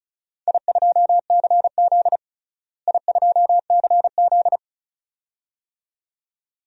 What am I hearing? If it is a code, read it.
Morse code, heard as a keyed tone, about 34 wpm, 700 Hz. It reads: I2CZ I2CZ
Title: DXCC-100 Call Signs - 4 Characters - Repeated 2x - 35wpm